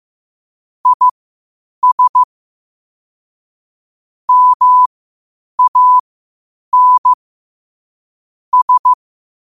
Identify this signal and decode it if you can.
Morse code, heard as a keyed tone, about 15 wpm, spaced as for 8 wpm, 1000 Hz. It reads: IS MAN S
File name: qotdfiles / 8wpm